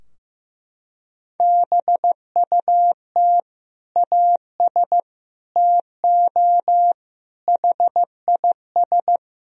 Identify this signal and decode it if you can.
Morse code, heard as a keyed tone, about 15 wpm, 700 Hz. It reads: BUT AS TO HIS
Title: Sentences from Top 1000 Words - Review Part 2 - 15wpm